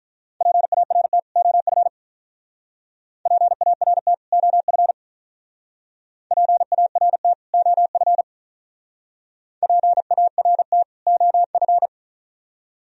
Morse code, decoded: PART OF PART OF PART OF PART OF

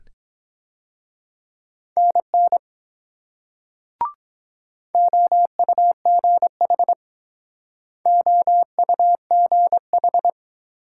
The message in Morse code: NN OUGH OUGH

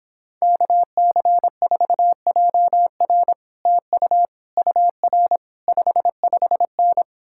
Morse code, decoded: KC4JR TU UR 55N

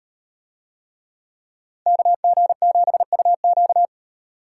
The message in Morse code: KG7UQ

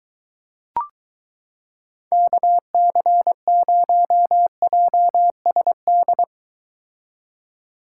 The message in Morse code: KC0JSD

